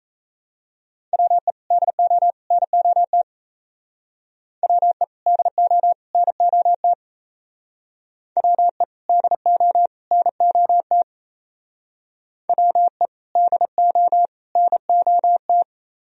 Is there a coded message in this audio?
Yes: Morse code from 1.1 to 15.6 s, heard as a keyed tone, about 36 wpm, 700 Hz. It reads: WE DO NOT WE DO NOT WE DO NOT WE DO NOT